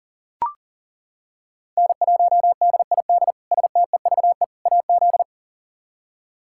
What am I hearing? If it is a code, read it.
Morse code, heard as a keyed tone, about 40 wpm, 700 Hz. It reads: N1DID STEVE AZ
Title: K1USN SST QSO Practice - 40wpm